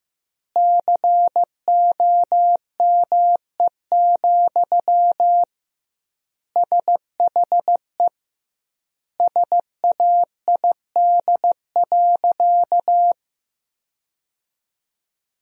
Morse code, heard as a keyed tone, about 15 wpm, 700 Hz. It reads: COME, SHE SAID.
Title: 2x Word-Spacing - Sentences from Top 100 Words - 15wpm